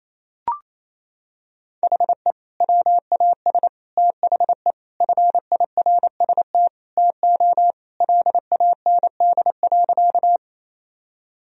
Morse code, heard as a keyed tone, about 28 wpm, 700 Hz. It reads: HE WAS THE FIRST TO LAND.
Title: Sentences from Top 1000 Words - Review Part 1 - 28wpm